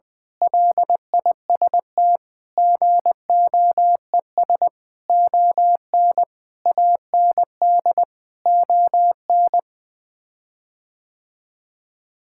Morse code, heard as a keyed tone, about 20 wpm, 700 Hz. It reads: LIST GOES ON AND ON
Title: Sets of 6 Words - 20wpm